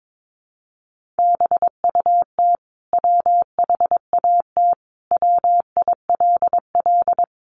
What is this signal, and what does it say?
Morse code, heard as a keyed tone, about 22 wpm, 700 Hz. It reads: BUT WHAT WILL
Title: Sentences from Top 100 Words 22wpm